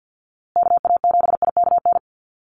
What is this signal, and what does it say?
Morse code, heard as a keyed tone, about 50 wpm, 700 Hz. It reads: KA7IKN
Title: USA Call Signs Repeated 2x 50wpm